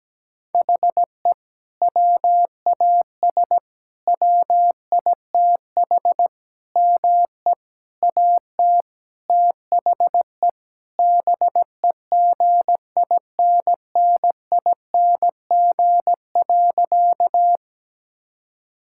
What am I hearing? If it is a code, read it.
Morse code, heard as a keyed tone, about 17 wpm, 700 Hz. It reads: HE WAS WITH ME AT THE BEGINNING.
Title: Sentences from Top 1000 Words - Review Part 5 - 17wpm